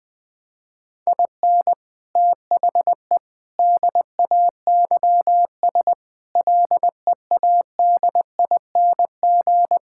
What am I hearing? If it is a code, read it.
Morse code, heard as a keyed tone, about 20 wpm, 700 Hz. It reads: IN THE DAYS LEADING
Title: Sentences with 7-Word Phrases - Spoken Beginning - 20wpm